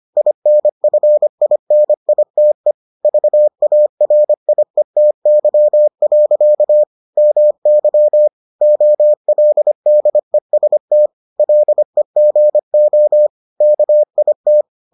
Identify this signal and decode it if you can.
Morse code, heard as a keyed tone, about 25 wpm, 600 Hz. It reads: INFINITE VARIETY. MY OLDEST LEGO KIT